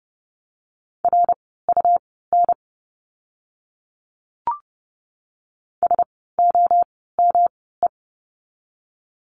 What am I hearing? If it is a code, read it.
Morse code, heard as a keyed tone, about 30 wpm, spaced as for 17 wpm, 700 Hz. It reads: RUN SOME
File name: course / 30:10wpm / 237 Lesson 38f - Letter q - Words 30wpm